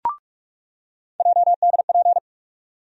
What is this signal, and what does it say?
Morse code, heard as a keyed tone, about 46 wpm, 700 Hz. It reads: JDP
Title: USA Call Sign Postfixes - 3 Letters 45wpm